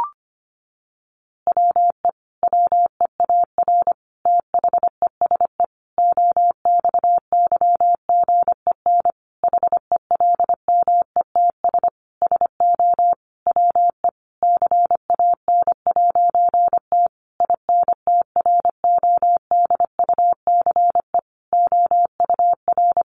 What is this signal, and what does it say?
Morse code, heard as a keyed tone, about 25 wpm, 700 Hz. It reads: WE WEAR THESE OXYGEN HELMETS SO WE CAN'T INTRODUCE OUR
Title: A Sound Of Thunder 25wpm